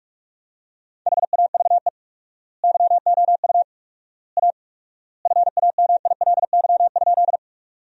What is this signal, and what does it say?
Morse code, heard as a keyed tone, about 45 wpm, 700 Hz. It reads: HAVE YOU A FAMILY?
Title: 4x Word-Spacing - Sentences from Top 400 Words - 45wpm